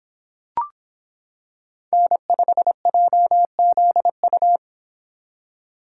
Morse code, heard as a keyed tone, about 26 wpm, 700 Hz. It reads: N5JZU